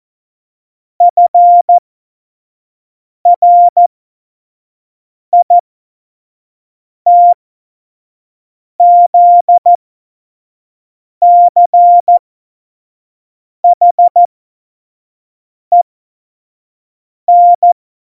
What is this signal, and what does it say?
Morse code, heard as a keyed tone, about 14 wpm, spaced as for 5 wpm, 700 Hz. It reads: FRITZCHEN
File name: Morsekurs / Witze / Witz 04. Juni 2019, 25 F70